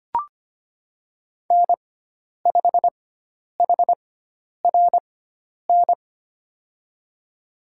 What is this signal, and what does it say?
Morse code, heard as a keyed tone, about 25 wpm, spaced as for 10 wpm, 700 Hz. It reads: N5HRN
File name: Lesson 16g - Letter c - Call Signs 25wpm